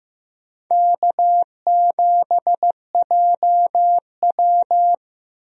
Morse code, heard as a keyed tone, about 15 wpm, 700 Hz. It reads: K7JW
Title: USA Call Signs - 4 Characters - 15wpm